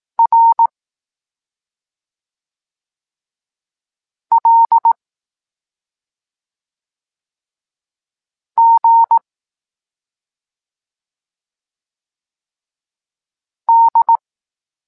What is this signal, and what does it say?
Morse code, heard as a keyed tone, about 18 wpm, 920 Hz. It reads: R L G D